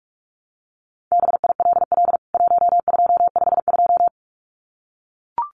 Morse code, heard as a keyed tone, about 44 wpm, 700 Hz. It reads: BILL 1252
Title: CWT Exchange 45wpm